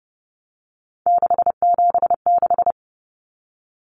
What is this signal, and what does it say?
Morse code, heard as a keyed tone, about 30 wpm, 700 Hz. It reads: 676
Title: Numbers - 3-Digit 30wpm